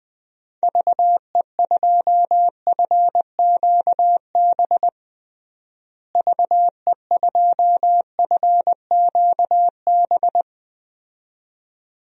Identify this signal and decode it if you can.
Morse code, heard as a keyed tone, about 20 wpm, 700 Hz. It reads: VE2FQB VE2FQB
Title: DXCC-100 Call Signs - 6 Characters - Repeated 2x - 20wpm